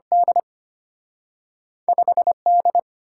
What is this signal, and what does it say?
Morse code, heard as a keyed tone, about 34 wpm, spaced as for 30 wpm, 690 Hz. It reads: D 5D
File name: practice / 25wpm / Speed-Racing - Two Letter-Number Combinations - 25wpm